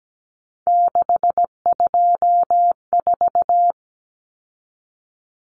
Morse code, heard as a keyed tone, about 17 wpm, 700 Hz. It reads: 624
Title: Numbers - 3-Digit 17wpm